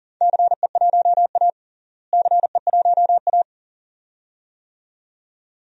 Morse code, heard as a keyed tone, about 40 wpm, 700 Hz. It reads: CE1A CE1A
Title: DXCC-100 Call Signs - 4 Characters - Repeated 2x - 40wpm